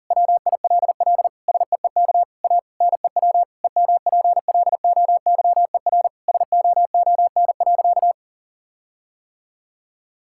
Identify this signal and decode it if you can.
Morse code, heard as a keyed tone, about 40 wpm, 700 Hz. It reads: WILL SEEK A NEW EMPLOYER SOON.
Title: Sentences from Top 1600 Words 40wpm